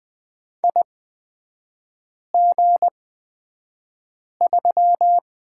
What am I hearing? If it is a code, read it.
Morse code, heard as a keyed tone, about 20 wpm, spaced as for 5 wpm, 700 Hz. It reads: IG3
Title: 236 Lesson 38e - Letter q - Three Characters 20wpm